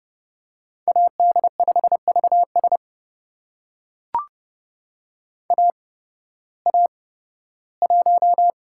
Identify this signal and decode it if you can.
Morse code, heard as a keyed tone, about 30 wpm, spaced as for 16 wpm, 700 Hz. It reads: AD5VS AA1
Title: USA Call Signs - 8x Character Spacing - 30wpm